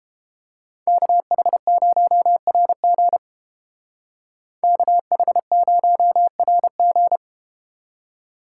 Morse code, heard as a keyed tone, about 32 wpm, 700 Hz. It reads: KH0RG KH0RG